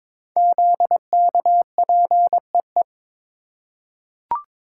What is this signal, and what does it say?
Morse code, heard as a keyed tone, about 22 wpm, 700 Hz. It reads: ZKPEE
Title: Code Groups - 5 Letters - 22wpm